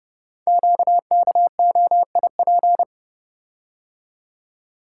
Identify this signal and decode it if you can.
Morse code, heard as a keyed tone, about 30 wpm, 690 Hz. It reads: QKOIP
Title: Code Groups - 5 Letters - 30wpm